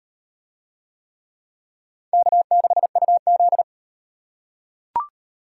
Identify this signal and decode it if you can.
Morse code, heard as a keyed tone, about 35 wpm, 700 Hz. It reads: K6UZ